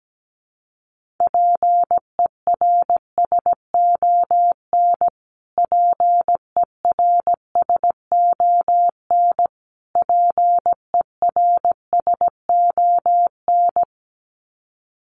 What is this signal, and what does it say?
Morse code, heard as a keyed tone, about 17 wpm, 700 Hz. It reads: PERSON PERSON PERSON
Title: Top 500 Words Repeated 17wpm